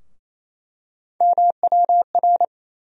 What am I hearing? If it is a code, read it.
Morse code, heard as a keyed tone, about 27 wpm, 700 Hz. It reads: MWR
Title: USA Call Sign Postfixes - 3 Letters 28wpm